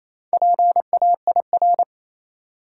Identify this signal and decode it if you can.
Morse code, heard as a keyed tone, about 28 wpm, 700 Hz. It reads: PAIR
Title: Four Letter Words - 28wpm